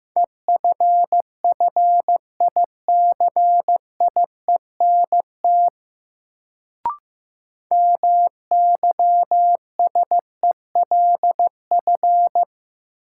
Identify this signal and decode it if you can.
Morse code, heard as a keyed tone, about 15 wpm, 700 Hz. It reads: EFFICIENT MYSELF
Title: Sentences from Top 1300 Words - New Word Drill Alt - 15wpm